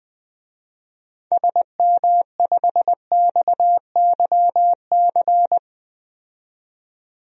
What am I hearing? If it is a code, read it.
Morse code, heard as a keyed tone, about 20 wpm, 700 Hz. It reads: SM5XYC
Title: DXCC-100 Call Signs - 6 Characters - 20wpm